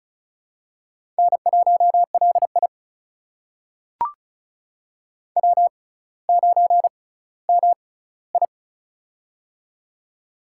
Morse code, heard as a keyed tone, about 34 wpm, spaced as for 18 wpm, 700 Hz. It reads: N1LI W9MI
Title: USA Call Signs - 4 Characters - 6x Character Spacing - 35wpm